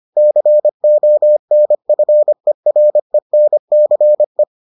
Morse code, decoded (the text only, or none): CONFERENCE